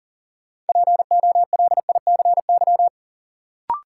Morse code, heard as a keyed tone, about 40 wpm, 700 Hz. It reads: POLICY
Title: Sentences from Top 1000 Words - Review Part 5 - New Word Drill - 40wpm